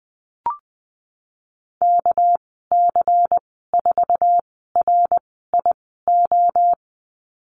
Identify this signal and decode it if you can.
Morse code, heard as a keyed tone, about 20 wpm, spaced as for 15 wpm, 700 Hz. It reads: KC4RIO
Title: 219 Lesson 35g - Number 0 - Call Signs 20wpm